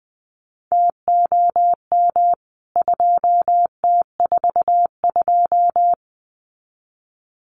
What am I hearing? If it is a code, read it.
Morse code, heard as a keyed tone, about 20 wpm, 700 Hz. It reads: TOM 2T42
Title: CWT Exchange 20wpm